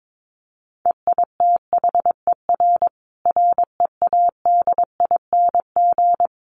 Morse code, heard as a keyed tone, about 22 wpm, 700 Hz. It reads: EITHER READING
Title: Sentences from Top 1400 Words 22wpm